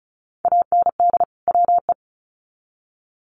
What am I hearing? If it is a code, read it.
Morse code, heard as a keyed tone, about 34 wpm, 700 Hz. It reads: AND WE
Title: Sets of 2 Words 35wpm